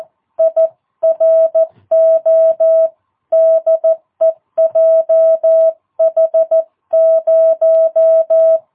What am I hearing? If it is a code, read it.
Morse code, heard as a keyed tone, about 14 wpm, 650 Hz. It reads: IRO DE JH0
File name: jh8cbh / 2022cw